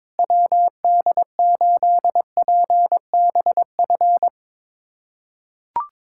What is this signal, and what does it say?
Morse code, heard as a keyed tone, about 22 wpm, 700 Hz. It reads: WD8PBF
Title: USA Callsign DE Callsign - 6 Characters - 22wpm